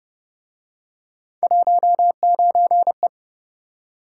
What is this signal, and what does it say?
Morse code, heard as a keyed tone, about 30 wpm, 700 Hz. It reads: 19E